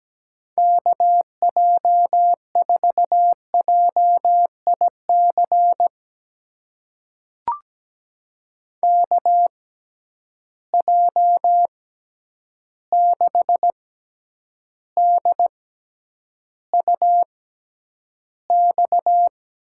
Morse code, heard as a keyed tone, about 17 wpm, spaced as for 9 wpm, 700 Hz. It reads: KJ4JIC KJ6DUX
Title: USA Call Signs - 6 Characters - 6x Character Spacing - 17wpm